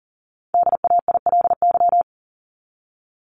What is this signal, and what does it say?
Morse code, heard as a keyed tone, about 40 wpm, 700 Hz. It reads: DAILY
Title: LY Suffix - Top 100 Words with Base Words 40wpm